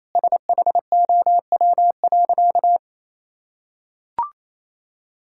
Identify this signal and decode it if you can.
Morse code, heard as a keyed tone, about 27 wpm, 700 Hz. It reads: SHOW.
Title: Sentences from Top 1000 Words - Review Part 6 - 28wpm